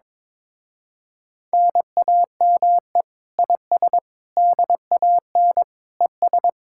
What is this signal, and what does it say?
Morse code, heard as a keyed tone, about 22 wpm, 700 Hz. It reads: NAME IS DAN ES
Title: Speed-Racing Common QSO Phrases - 22wpm